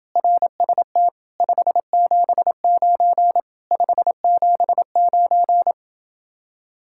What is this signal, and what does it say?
Morse code, heard as a keyed tone, about 29 wpm, 700 Hz. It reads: RST 579 579